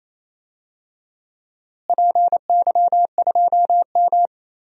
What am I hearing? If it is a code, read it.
Morse code, heard as a keyed tone, about 28 wpm, 700 Hz. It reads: PY2M
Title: DXCC-100 Call Signs - 4 Characters - Repeated 2x - 28wpm